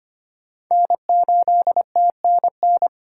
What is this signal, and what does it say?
Morse code, heard as a keyed tone, about 25 wpm, 700 Hz. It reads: N8TNN
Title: USA Call Signs - 5 Characters - 4x Character Spacing - 25wpm